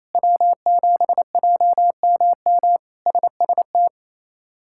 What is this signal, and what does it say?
Morse code, heard as a keyed tone, about 28 wpm, 700 Hz. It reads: W7JMM SST
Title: K1USN SST QSO Practice - 28wpm